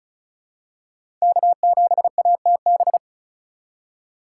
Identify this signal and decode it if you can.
Morse code, heard as a keyed tone, about 36 wpm, 700 Hz. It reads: K7ATB
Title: USA Call Signs - 5 Characters - 2x Character Spacing - 35wpm